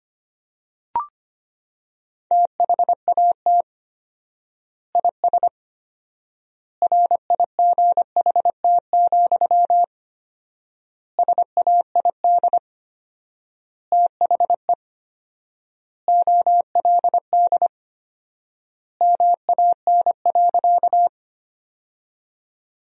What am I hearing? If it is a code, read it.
Morse code, heard as a keyed tone, about 25 wpm, 700 Hz. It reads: THAT IS RIGHT, SAID THE OLD MAN.
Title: 4x Word-Spacing - Sentences from Top 200 Words - 25wpm